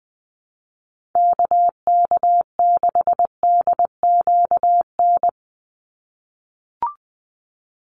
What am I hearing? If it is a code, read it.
Morse code, heard as a keyed tone, about 19 wpm, 700 Hz. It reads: KK6DQN
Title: USA Call Signs 20wpm